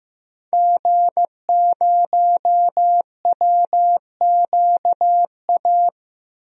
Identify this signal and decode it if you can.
Morse code, heard as a keyed tone, about 15 wpm, 700 Hz. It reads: G0WQA
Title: DXCC-100 Call Signs - 5 Characters - Repeated 2x - 15wpm